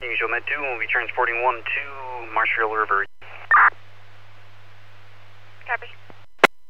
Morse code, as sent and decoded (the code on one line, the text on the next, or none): none